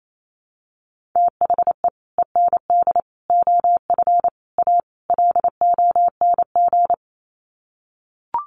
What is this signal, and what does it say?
Morse code, heard as a keyed tone, about 28 wpm, 700 Hz. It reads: THE END OF A LONG
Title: Sets of 5 Words Encore2 28wpm